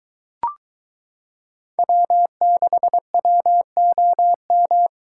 Morse code, heard as a keyed tone, about 23 wpm, 700 Hz. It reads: W6WOM